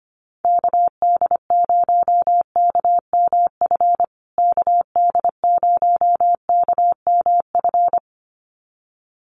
Morse code, heard as a keyed tone, about 25 wpm, 700 Hz. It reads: KD0KMF KD0KMF
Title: USA Call Signs Repeated 2x 25wpm